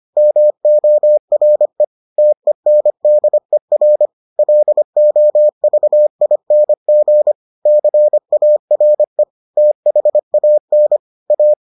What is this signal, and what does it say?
Morse code, heard as a keyed tone, about 25 wpm, 600 Hz. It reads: MORE TENDER LOVING CARE THAN A